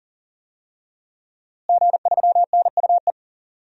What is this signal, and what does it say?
Morse code, heard as a keyed tone, about 40 wpm, 700 Hz. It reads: G3NUE
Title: DXCC-100 Call Signs - 5 Characters - Repeated 3x - 40wpm